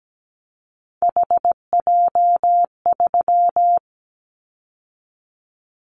Morse code, heard as a keyed tone, about 17 wpm, 700 Hz. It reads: HJ3
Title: Callsign Prefixes for Top 100 DXCC - 17wpm